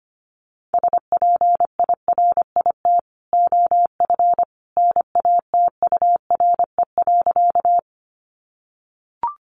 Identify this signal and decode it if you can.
Morse code, heard as a keyed tone, about 25 wpm, 700 Hz. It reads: SPIRIT OF NATURE.
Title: Sentences from Top 1400 Words 25wpm